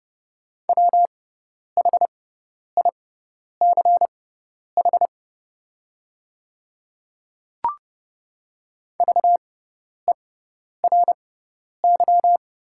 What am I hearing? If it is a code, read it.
Morse code, heard as a keyed tone, about 30 wpm, spaced as for 10 wpm, 700 Hz. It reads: WHICH VERY
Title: 6x Character Spacing - Top 200 Words - 30wpm